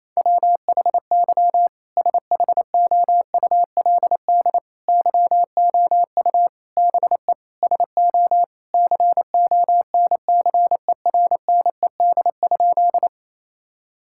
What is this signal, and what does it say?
Morse code, heard as a keyed tone, about 28 wpm, 700 Hz. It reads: WHY SHOULD YOU BE SO CONCERNED?
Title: Sentences from Top 1000 Words - Review Part 6 - 28wpm